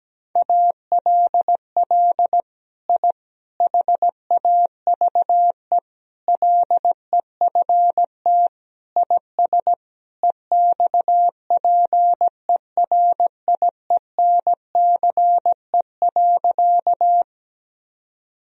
Morse code, heard as a keyed tone, about 17 wpm, 700 Hz. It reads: ALL I HAVE LEFT IS EXPERIENCE.
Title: Sentences from Top 1000 Words - Review Part 3 - 17wpm